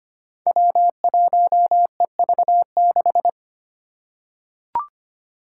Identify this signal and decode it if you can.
Morse code, heard as a keyed tone, about 25 wpm, 700 Hz. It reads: W1EV6
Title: Five Challenging Letter-Number Combinations 25wpm